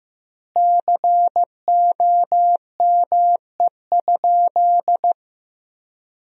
Morse code, heard as a keyed tone, about 16 wpm, 700 Hz. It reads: COME?
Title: 2x Word-Spacing - Sentences from Top 200 Words - 15wpm